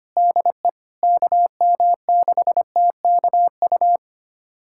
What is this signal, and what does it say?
Morse code, heard as a keyed tone, about 25 wpm, 700 Hz. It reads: DE KM6TKU
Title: USA Callsign DE Callsign 25wpm